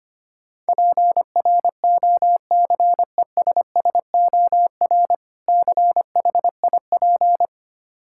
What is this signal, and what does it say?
Morse code, heard as a keyed tone, about 25 wpm, 700 Hz. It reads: PROCESSOR CHIP
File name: practice / 25wpm / Difficult Words - Speed-Racing with Hints - 25wpm